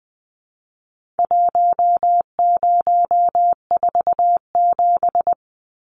Morse code, heard as a keyed tone, about 20 wpm, 700 Hz. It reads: 1047
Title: Numbers - Contest Style - 4-Digit 20wpm